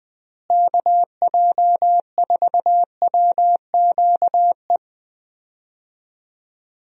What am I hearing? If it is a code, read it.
Morse code, heard as a keyed tone, about 20 wpm, 700 Hz. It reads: KJ4WQE
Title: USA Call Signs - 6 Characters - Repeated 2x - 20wpm